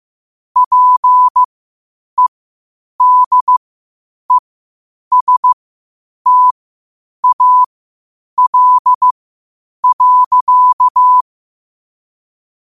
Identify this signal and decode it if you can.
Morse code, heard as a keyed tone, about 15 wpm, spaced as for 8 wpm, 1000 Hz. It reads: PEDESTAL.
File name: qotdfiles / 8wpm